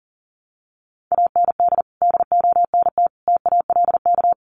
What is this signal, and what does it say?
Morse code, heard as a keyed tone, about 40 wpm, 700 Hz. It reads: AND DONT TALK
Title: Sentences from Top 400 Words 40wpm